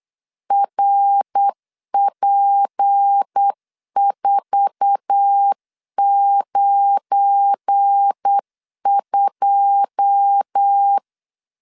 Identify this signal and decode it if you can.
Morse code, heard as a keyed tone, about 8 wpm, 800 Hz. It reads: RP492
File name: CW / DET / 1 9 1500C S A 8wpm 5637